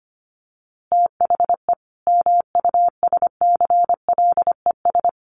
Morse code, heard as a keyed tone, about 25 wpm, 700 Hz. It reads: THE MUSCLES
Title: A Sound Of Thunder 25wpm